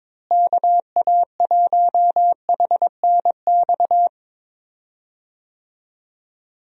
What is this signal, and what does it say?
Morse code, heard as a keyed tone, about 22 wpm, 700 Hz. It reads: KA1HNX
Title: USA Call Signs - 6 Characters - Repeated 2x - 22wpm